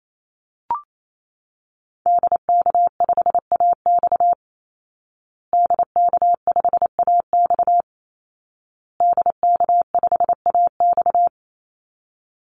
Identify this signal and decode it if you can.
Morse code, heard as a keyed tone, about 28 wpm, 700 Hz. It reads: DK5AX DK5AX DK5AX
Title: Call Signs Repeated 2x - 28wpm